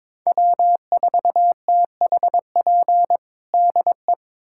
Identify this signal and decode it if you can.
Morse code, heard as a keyed tone, about 21 wpm, 700 Hz. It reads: W4THP DE
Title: USA Callsign DE Callsign - 5 Characters - 22wpm